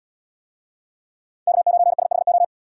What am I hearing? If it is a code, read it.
Morse code, heard as a keyed tone, about 78 wpm, 700 Hz. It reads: K9SHO